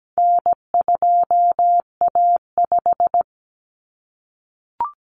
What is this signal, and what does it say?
Morse code, heard as a keyed tone, about 17 wpm, 700 Hz. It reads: N2A5
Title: Common Cut Numbers - 4-Digit 17wpm